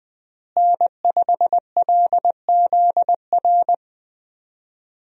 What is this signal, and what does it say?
Morse code, heard as a keyed tone, about 20 wpm, 700 Hz. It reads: N5LZR
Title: USA Call Signs - 5 Characters - Repeated 3x - 20wpm